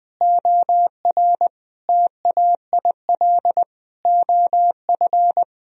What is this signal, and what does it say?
Morse code, heard as a keyed tone, about 20 wpm, 700 Hz. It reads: OR TAIL OF
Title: Sentences from Top 1000 Words - Review Part 3 - 20wpm